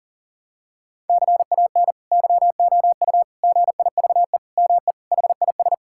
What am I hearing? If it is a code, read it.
Morse code, heard as a keyed tone, about 40 wpm, 700 Hz. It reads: CAN YOU GIVE ME HIS
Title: Sentences from Top 1000 Words - Review Part 6 - 40wpm